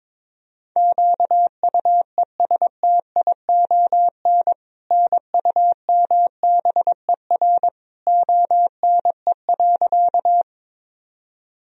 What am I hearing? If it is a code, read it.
Morse code, heard as a keyed tone, about 22 wpm, 700 Hz. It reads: QUESTION NUMBER ONE.
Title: Sentences from Top 1000 Words - Review Part 1 - 22wpm